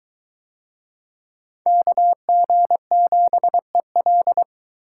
Morse code, heard as a keyed tone, about 22 wpm, 700 Hz. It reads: KG7EL